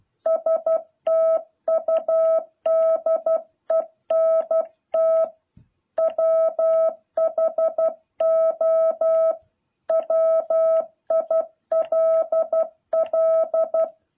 Morse code, heard as a keyed tone, about 12 wpm, 650 Hz. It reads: STUDENT WHO WILL
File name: jh8cbh / 2021cw